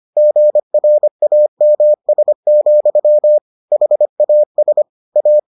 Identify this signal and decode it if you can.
Morse code, heard as a keyed tone, about 25 wpm, 600 Hz. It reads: GRAMS, HAS A